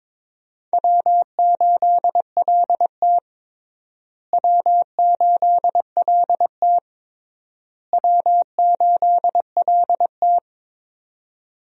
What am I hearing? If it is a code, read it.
Morse code, heard as a keyed tone, about 22 wpm, 700 Hz. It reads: W8LT W8LT W8LT
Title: USA Call Signs - 4 Characters - Repeated 3x - 22wpm